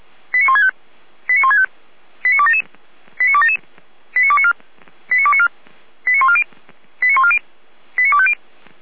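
Five-tone selective call (ZVEI), heard as quick runs of five tones, tones of about 70 ms. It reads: 89165 89165 89277 89277 89294 89294 89140 89140 89250
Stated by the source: Five-tone ZVEI selective call received on SDR.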